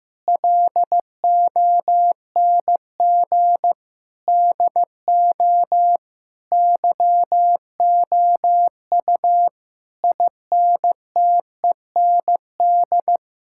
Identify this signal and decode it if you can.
Morse code, heard as a keyed tone, about 15 wpm, 700 Hz. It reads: LONG DO YOU INTEND
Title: Sentences from Top 1600 Words 15wpm